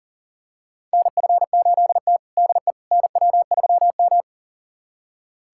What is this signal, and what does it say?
Morse code, heard as a keyed tone, about 40 wpm, 700 Hz. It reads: NF8T DE NW3M
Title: USA Callsign DE Callsign - 4 Characters - 40wpm